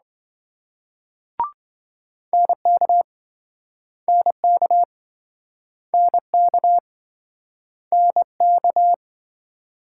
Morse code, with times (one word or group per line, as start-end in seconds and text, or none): 2.3-3.0 NK
4.1-4.8 NK
5.9-6.8 NK
7.9-8.9 NK